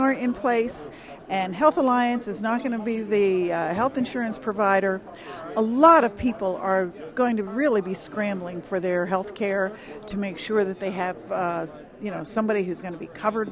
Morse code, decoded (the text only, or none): none